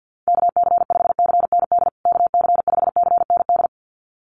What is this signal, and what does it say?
Morse code, heard as a keyed tone, about 50 wpm, 700 Hz. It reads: KC5CND KC5CND
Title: USA Call Signs Repeated 2x 50wpm